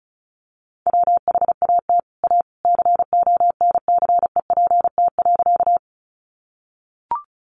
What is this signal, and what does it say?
Morse code, heard as a keyed tone, about 34 wpm, 700 Hz. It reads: WHAT A CONCEPT.
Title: Sentences from Top 1100 Words 35wpm